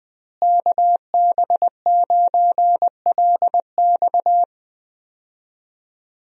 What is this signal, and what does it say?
Morse code, heard as a keyed tone, about 20 wpm, 700 Hz. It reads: KB9LX
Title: USA Call Signs - 5 Characters - Repeated 2x - 20wpm